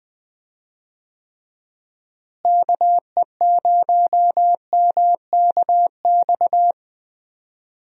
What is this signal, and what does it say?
Morse code, heard as a keyed tone, about 21 wpm, 700 Hz. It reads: KE0MKX
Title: USA Call Signs - 6 Characters - Repeated 3x - 20wpm